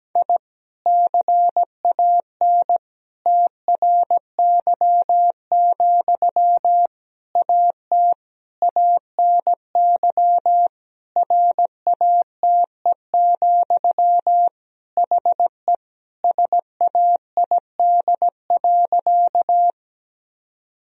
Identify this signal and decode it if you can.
Morse code, heard as a keyed tone, about 17 wpm, 700 Hz. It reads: I CAN TRY, AT ANY RATE, HE SAID.
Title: Sentences from Top 1000 Words - Review Part 4 - 17wpm